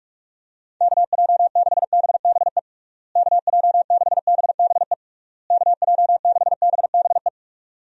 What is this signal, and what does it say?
Morse code, heard as a keyed tone, about 41 wpm, 700 Hz. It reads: KJ6BBE KJ6BBE KJ6BBE
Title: USA Call Signs - 6 Characters - Repeated 3x - 45wpm